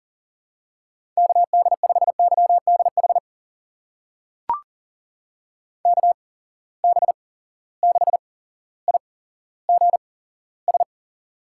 Morse code, heard as a keyed tone, about 39 wpm, spaced as for 17 wpm, 700 Hz. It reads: KD5YDH KB6IGS
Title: USA Call Signs - 6 Characters - 8x Character Spacing - 40wpm